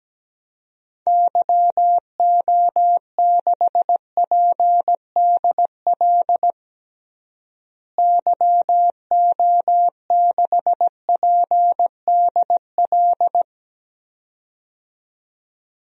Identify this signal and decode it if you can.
Morse code, heard as a keyed tone, about 17 wpm, 700 Hz. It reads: YO6PDL YO6PDL
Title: DXCC-100 Call Signs - 6 Characters - Repeated 2x - 17wpm